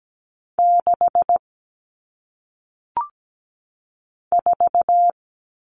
Morse code, heard as a keyed tone, about 17 wpm, 700 Hz. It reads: 6 4
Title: All Cut Numbers - 1-Digit 17wpm